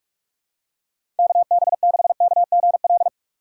Morse code, heard as a keyed tone, about 41 wpm, 700 Hz. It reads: KB6KGL
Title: USA Call Signs - 6 Characters - 3x Character Spacing - 45wpm